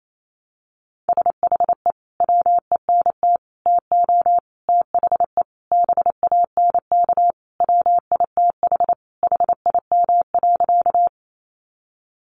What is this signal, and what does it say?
Morse code, heard as a keyed tone, about 28 wpm, 700 Hz. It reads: SHE WENT TO THE BANK WITH HIM.
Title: Sentences from Top 900 Words 28wpm